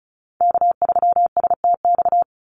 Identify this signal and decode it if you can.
Morse code, heard as a keyed tone, about 34 wpm, 700 Hz. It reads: K3STX
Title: Call Signs Repeated 3x - 35wpm